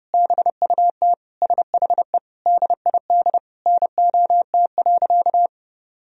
Morse code, heard as a keyed tone, about 32 wpm, 700 Hz. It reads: BUT SHE DID NOT.
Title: Sentences from Top 1000 Words - Review Part 1 - 30wpm